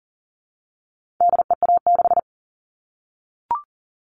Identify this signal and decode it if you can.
Morse code, heard as a keyed tone, about 40 wpm, 700 Hz. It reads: DEA6
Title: All Cut Numbers - 4-Digit 40wpm